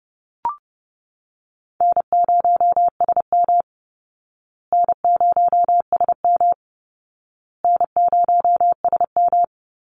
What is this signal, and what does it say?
Morse code, heard as a keyed tone, about 30 wpm, 700 Hz. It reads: N0SM N0SM N0SM
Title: Easy Call Signs Repeated 3x - 30wpm